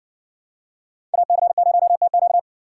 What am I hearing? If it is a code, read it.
Morse code, heard as a keyed tone, about 60 wpm, 700 Hz. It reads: AC0TQ